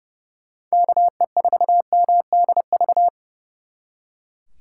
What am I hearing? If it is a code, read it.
Morse code, heard as a keyed tone, about 38 wpm, 700 Hz. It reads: KE4MDV
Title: Speed Racing - USA 2x3 Call Signs 30wpm